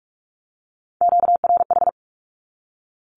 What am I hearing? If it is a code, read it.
Morse code, heard as a keyed tone, about 44 wpm, 700 Hz. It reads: QRH
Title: Q-Codes Repeated 3x 45wpm